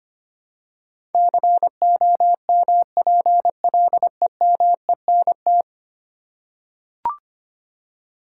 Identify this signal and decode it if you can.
Morse code, heard as a keyed tone, about 25 wpm, 700 Hz. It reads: COMPLEMENT
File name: Ten Letter Words - 25wpm